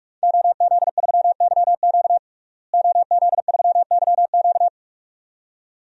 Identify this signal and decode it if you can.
Morse code, heard as a keyed tone, about 46 wpm, 700 Hz. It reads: OZ3YQ OZ3YQ
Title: DXCC-100 Call Signs - 5 Characters - Repeated 2x - 45wpm